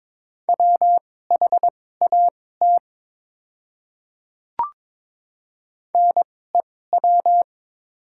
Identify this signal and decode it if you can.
Morse code, heard as a keyed tone, about 22 wpm, spaced as for 16 wpm, 700 Hz. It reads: WHAT NEW
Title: 2x Character Spacing - Top 300 Words - 22wpm